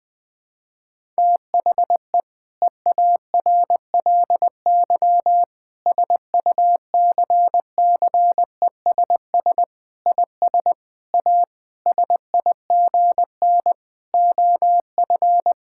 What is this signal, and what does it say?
Morse code, heard as a keyed tone, about 21 wpm, 700 Hz. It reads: THE EARLY SUCCESS IS A SIGN OF
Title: Sentences with 6-Word Phrases - 20wpm